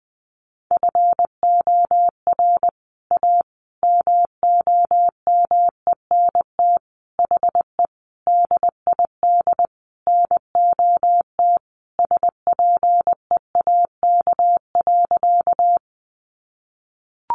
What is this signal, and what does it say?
Morse code, heard as a keyed tone, about 20 wpm, 700 Hz. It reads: FOR A MOMENT HE DID NOT SPEAK.
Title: Sentences from Top 1100 Words 20wpm